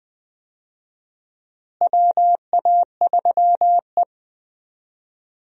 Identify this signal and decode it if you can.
Morse code, heard as a keyed tone, about 20 wpm, 700 Hz. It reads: WA3E
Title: USA Call Signs - 4 Characters - Repeated 2x - 20wpm